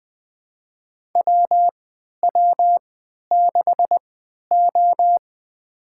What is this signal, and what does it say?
Morse code, heard as a keyed tone, about 21 wpm, spaced as for 11 wpm, 700 Hz. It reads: WW6O
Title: USA Call Signs - 4 Characters - 3x Character Spacing - 20wpm